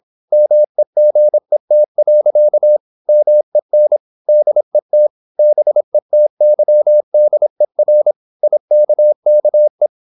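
Morse code, heard as a keyed tone, about 26 wpm, 600 Hz. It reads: MEGET. MEN DET BETYDER IKKE
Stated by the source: High-speed (QRQ) Morse at 26 wpm.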